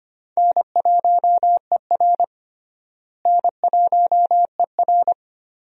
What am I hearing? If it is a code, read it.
Morse code, heard as a keyed tone, about 25 wpm, 700 Hz. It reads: N1ER N1ER
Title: USA Call Signs - 4 Characters - Repeated 3x - 25wpm